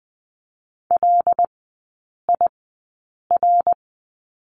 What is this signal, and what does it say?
Morse code, heard as a keyed tone, about 20 wpm, spaced as for 8 wpm, 700 Hz. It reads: LIR
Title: 121 Lesson 20e - Number 3 - Three Characters 20wpm